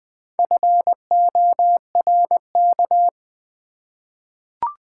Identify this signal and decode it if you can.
Morse code, heard as a keyed tone, about 21 wpm, 700 Hz. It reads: FORK
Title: Easy Words with Hints - 20wpm